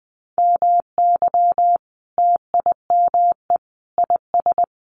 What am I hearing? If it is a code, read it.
Morse code, heard as a keyed tone, about 20 wpm, 700 Hz. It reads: MY TIME IS
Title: Sentences from Top 1100 Words 20wpm